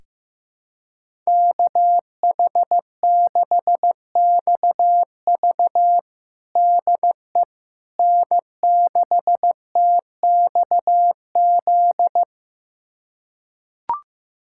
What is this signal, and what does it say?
Morse code, heard as a keyed tone, about 15 wpm, 700 Hz. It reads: KH6XV DE N6TXZ
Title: USA Callsign DE Callsign - 5 Characters - 15wpm